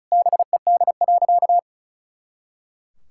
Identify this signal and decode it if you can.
Morse code, heard as a keyed tone, about 36 wpm, 700 Hz. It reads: BED.
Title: Pangram Challenge v2 - 35wpm